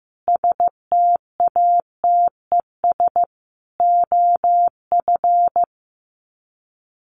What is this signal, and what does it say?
Morse code, heard as a keyed tone, about 15 wpm, 700 Hz. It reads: STATES OF
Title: Sets of 5 Words Encore 15wpm